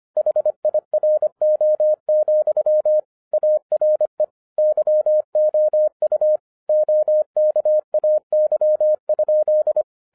Morse code, heard as a keyed tone, about 25 wpm, 610 Hz. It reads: HIRO, ARE YOU OKAY?
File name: 2011-07-mp3 / n34 THE 3.11 2 25wpm